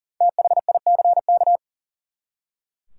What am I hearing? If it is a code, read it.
Morse code, heard as a keyed tone, about 40 wpm, 700 Hz. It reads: THICK
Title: Five Letter Words - 40wpm